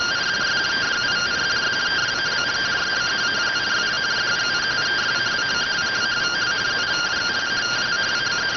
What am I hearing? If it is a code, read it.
Two-tone teleprinter signal, about 170 Hz shift.